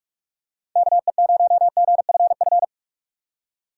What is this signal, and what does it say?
Morse code, heard as a keyed tone, about 46 wpm, 700 Hz. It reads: KE0GFF
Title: USA Call Signs - 6x Character Spacing - 45wpm